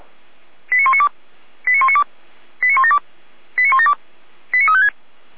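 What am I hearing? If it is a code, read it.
Five-tone selective call (ZVEI), heard as quick runs of five tones, tones of about 70 ms. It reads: 89192 89192 89172 89172 89356
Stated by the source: ZVEI five-tone call 89340 received by SDR.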